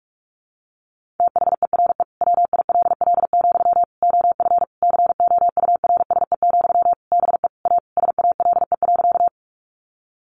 Morse code, heard as a keyed tone, about 44 wpm, 700 Hz. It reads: THERE WILL, OF COURSE, BE A SALE.
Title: Sentences from Top 1400 Words 45wpm